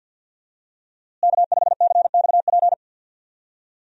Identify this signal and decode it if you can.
Morse code, heard as a keyed tone, about 51 wpm, 690 Hz. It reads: K5CXP
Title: USA Call Signs - 5 Characters - Repeated 3x - 50wpm